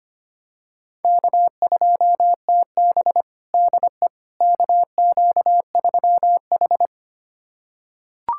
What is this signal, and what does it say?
Morse code, heard as a keyed tone, about 25 wpm, 700 Hz. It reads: K2TB DE KQ3H
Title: USA Callsign DE Callsign - 4 Characters - 25wpm